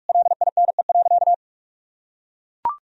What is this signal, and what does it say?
Morse code, heard as a keyed tone, about 41 wpm, 700 Hz. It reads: LINE.
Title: Sentences from Top 1000 Words - Review Part 1 - 45wpm